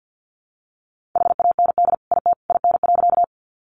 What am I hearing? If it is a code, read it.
Morse code, heard as a keyed tone, about 48 wpm, 700 Hz. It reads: HAND IT IN.
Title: Sentences from Top 200 Words 50wpm